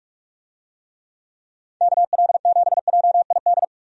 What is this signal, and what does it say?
Morse code, heard as a keyed tone, about 41 wpm, 700 Hz. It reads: KL7JID
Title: DXCC-100 Call Signs - 6 Characters - Repeated 2x - 45wpm